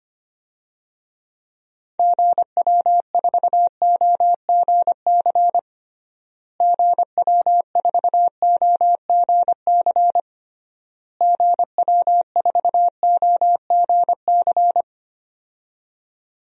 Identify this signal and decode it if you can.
Morse code, heard as a keyed tone, about 25 wpm, 700 Hz. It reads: GW4OGC GW4OGC GW4OGC
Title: DXCC-100 Call Signs - 6 Characters - Repeated 3x - 25wpm